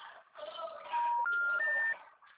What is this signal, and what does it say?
Special information tone, with noise.